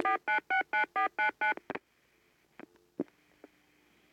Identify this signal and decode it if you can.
DTMF, heard as telephone touch tones, tones of about 120 ms. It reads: 4969499